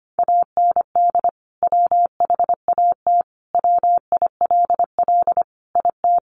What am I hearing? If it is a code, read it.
Morse code, heard as a keyed tone, about 25 wpm, 700 Hz. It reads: AND WHAT WILL IT
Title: Sentences from Top 1200 Words 25wpm